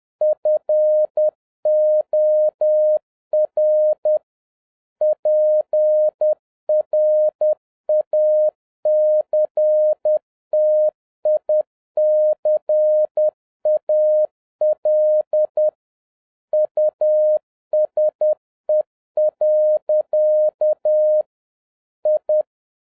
Morse code, heard as a keyed tone, about 10 wpm, 610 Hz. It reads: FOR PRACTICAL USE. I
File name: 2012-01-mp3 / n09-ITO JR7OEF-10wpm